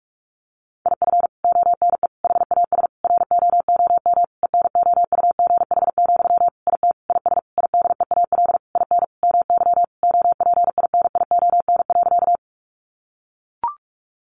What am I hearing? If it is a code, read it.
Morse code, heard as a keyed tone, about 44 wpm, 700 Hz. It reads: IF ONE HAS ROOM ENOUGH, IT IS IDEAL IN MY OPINION.
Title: Sentences from Top 1400 Words 45wpm